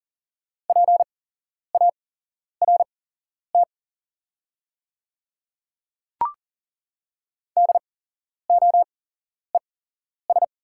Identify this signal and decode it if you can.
Morse code, heard as a keyed tone, about 40 wpm, spaced as for 11 wpm, 690 Hz. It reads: PART DOES
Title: 8x Character Spacing - Top 200 Words - 40wpm